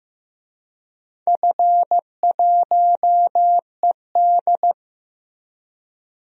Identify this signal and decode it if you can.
Morse code, heard as a keyed tone, about 15 wpm, 700 Hz. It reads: F1ED
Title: DXCC-100 Call Signs - 4 Characters - Repeated 3x - 15wpm